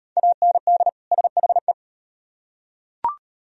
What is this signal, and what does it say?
Morse code, heard as a keyed tone, about 38 wpm, 700 Hz. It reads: AND SHE